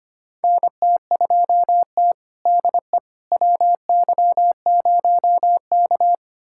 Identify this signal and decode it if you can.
Morse code, heard as a keyed tone, about 25 wpm, 700 Hz. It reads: NT2T DE WY0K
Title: USA Callsign DE Callsign - 4 Characters - 25wpm